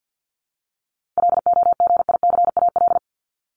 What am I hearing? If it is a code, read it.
Morse code, heard as a keyed tone, about 50 wpm, 700 Hz. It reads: LOGICAL
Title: Top 1501-1750 Words 50wpm